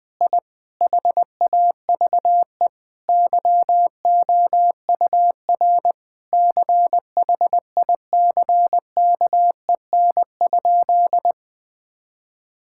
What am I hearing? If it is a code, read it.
Morse code, heard as a keyed tone, about 20 wpm, 700 Hz. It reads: I HAVE YOUR CHICKEN?
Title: Sentences from Top 1000 Words - Review Part 5 - 20wpm